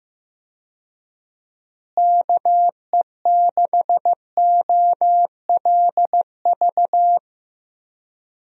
Morse code, heard as a keyed tone, about 15 wpm, 700 Hz. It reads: KE6OLV
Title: USA Call Signs - 6 Characters - Repeated 3x - 15wpm